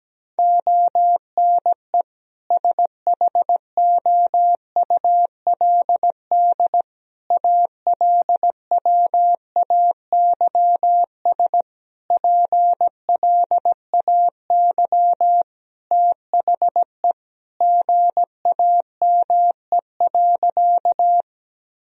Morse code, heard as a keyed tone, about 17 wpm, 700 Hz. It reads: ONE SHOULD ALWAYS PLAY THE GAME.
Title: Sentences from Top 1000 Words - Review Part 3 - 17wpm